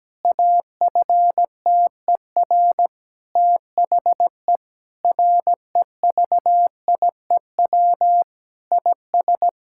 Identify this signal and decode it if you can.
Morse code, heard as a keyed tone, about 17 wpm, 700 Hz. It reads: AFTER THE REVIEW IS
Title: Sentences from Top 1000 Words - Review Part 6 - 17wpm